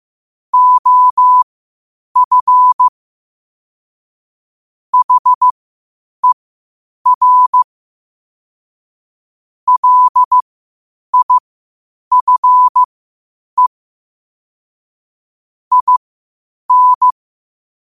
Morse code, heard as a keyed tone, about 15 wpm, spaced as for 8 wpm, 1000 Hz. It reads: OF HER LIFE IN
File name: qotdfiles / 8wpm